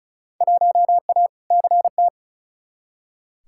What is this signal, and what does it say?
Morse code, heard as a keyed tone, about 34 wpm, 700 Hz. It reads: 1A CT
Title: Field Day Exchanges Encore - 35wpm